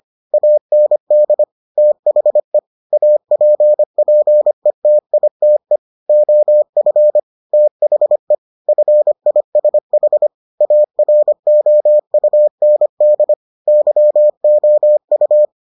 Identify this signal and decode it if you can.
Morse code, heard as a keyed tone, about 25 wpm, 600 Hz. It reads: AND THE APPETITE OF THE FISH AROUND YOU